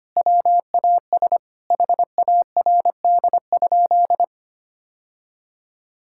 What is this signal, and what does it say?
Morse code, heard as a keyed tone, about 25 wpm, 700 Hz. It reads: WAS HARD?
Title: Sentences from Top 1000 Words - Review Part 5 - 25wpm